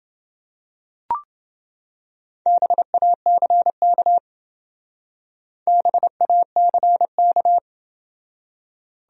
Morse code, heard as a keyed tone, about 29 wpm, 700 Hz. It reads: BACK BACK